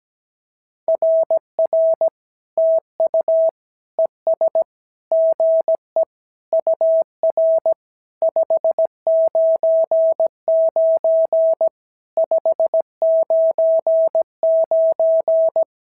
Morse code, heard as a keyed tone, about 17 wpm, 650 Hz. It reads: RR TU ES GE UR 599 599
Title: SOTA QSO Practice 17wpm